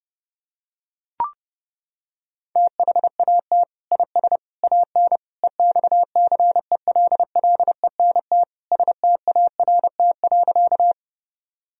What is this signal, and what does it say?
Morse code, heard as a keyed tone, about 32 wpm, 700 Hz. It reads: THAT IS AN EXCELLENT START.
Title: Sentences from Top 1000 Words - Review Part 1 - 30wpm